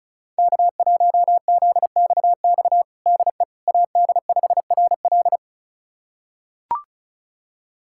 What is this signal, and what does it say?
Morse code, heard as a keyed tone, about 36 wpm, 700 Hz. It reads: K1ZXX DE AD5RL
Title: USA Callsign DE Callsign - 5 Characters - 35wpm